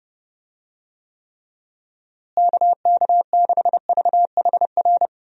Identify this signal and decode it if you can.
Morse code, heard as a keyed tone, about 30 wpm, 700 Hz. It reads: KK6VHR
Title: USA Call Signs - 6 Characters - Repeated 3x - 30wpm